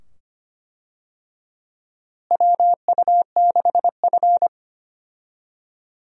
Morse code, heard as a keyed tone, about 25 wpm, 700 Hz. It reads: WU6F
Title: USA Call Signs - 4 Characters - Repeated 2x - 25wpm